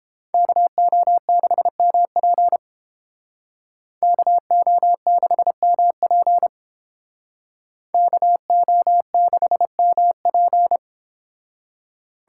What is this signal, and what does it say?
Morse code, heard as a keyed tone, about 29 wpm, 700 Hz. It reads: KO6MP KO6MP KO6MP